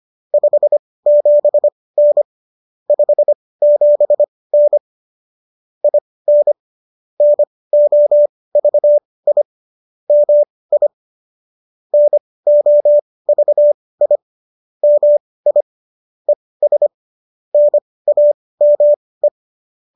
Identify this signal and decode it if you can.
Morse code, heard as a keyed tone, about 25 wpm, spaced as for 18 wpm, 590 Hz. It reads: 57N 57N IN NOVI MI NOVI MI ES NAME